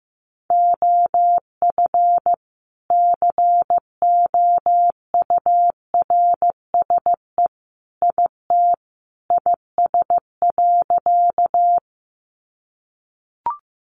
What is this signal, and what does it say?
Morse code, heard as a keyed tone, about 15 wpm, 700 Hz. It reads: OF COURSE IT IS.
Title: Sentences from Top 500 Words 15wpm